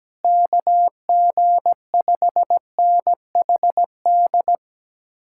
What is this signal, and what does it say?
Morse code, heard as a keyed tone, about 17 wpm, 700 Hz. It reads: KG5NHD
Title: Speed-Racing Common QSO Phrases - 17wpm